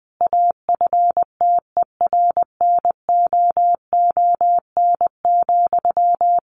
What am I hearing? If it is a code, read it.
Morse code, heard as a keyed tone, about 20 wpm, 700 Hz. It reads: AFTERNOON,
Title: Sentences from Top 1500 Words 20wpm